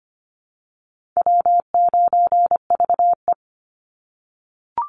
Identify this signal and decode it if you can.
Morse code, heard as a keyed tone, about 25 wpm, 700 Hz. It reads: W9VE
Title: Call Signs - 25wpm